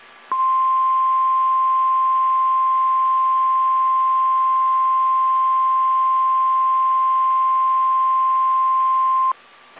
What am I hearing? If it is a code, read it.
Emergency Alert System (SAME) tones (the 1050 Hz attention tone). It is heard from 0.3 to 9.4 s.
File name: eas / KSDZKDJL / KSDZKDJL